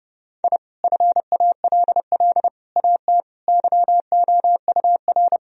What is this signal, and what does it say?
Morse code, heard as a keyed tone, about 30 wpm, 700 Hz. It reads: I FALL AT YOUR
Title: Sentences from Top 1000 Words - Review Part 2 - 30wpm